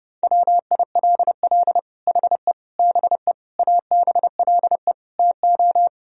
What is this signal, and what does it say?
Morse code, heard as a keyed tone, about 30 wpm, 700 Hz. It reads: WILL HE BE ABLE TO
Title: Sentences from Top 1000 Words - Review Part 3 - 30wpm